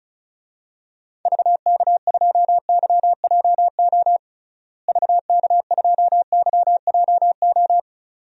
Morse code, heard as a keyed tone, about 34 wpm, 700 Hz. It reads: VK2YJO VK2YJO
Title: DXCC-100 Call Signs - 6 Characters - Repeated 2x - 35wpm